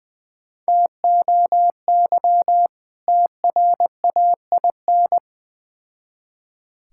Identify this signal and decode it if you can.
Morse code, heard as a keyed tone, about 21 wpm, 700 Hz. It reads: TOY TRAIN
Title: Difficult Words with Hints - 20wpm